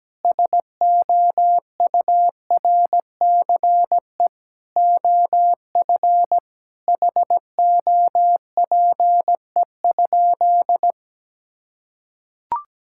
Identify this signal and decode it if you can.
Morse code, heard as a keyed tone, about 17 wpm, 700 Hz. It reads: SOURCE OF HOPE?
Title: Sentences from Top 1000 Words - Review Part 5 - 17wpm